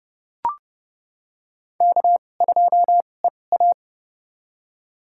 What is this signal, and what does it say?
Morse code, heard as a keyed tone, about 32 wpm, spaced as for 22 wpm, 700 Hz. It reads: K2EA
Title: USA Call Signs - 2x Character Spacing - 30wpm